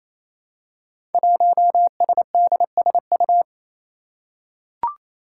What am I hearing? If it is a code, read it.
Morse code, heard as a keyed tone, about 29 wpm, 700 Hz. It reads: 1SDSU
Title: Five Challenging Letter-Number Combinations 28wpm